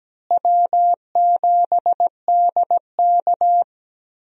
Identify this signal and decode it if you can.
Morse code, heard as a keyed tone, about 17 wpm, 700 Hz. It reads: W7DK
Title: USA Call Signs - 4 Characters - Repeated 2x - 17wpm